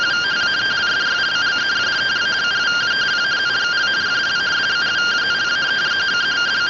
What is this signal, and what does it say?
Two-tone teleprinter signal, about 170 Hz shift.